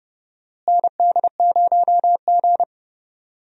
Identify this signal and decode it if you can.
Morse code, heard as a keyed tone, about 30 wpm, 700 Hz. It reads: ND0G
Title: USA Call Signs - 4 Characters - Repeated 2x - 30wpm